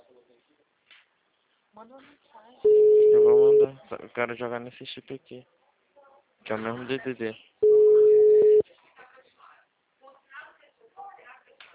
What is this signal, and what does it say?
Ringback tone (Europe, 425 Hz), with speech.